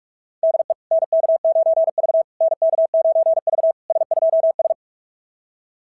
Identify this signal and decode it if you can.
Morse code, heard as a keyed tone, about 48 wpm, 660 Hz. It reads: DE NK9V NK9V S2S
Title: SOTA QSO Practice 45wpm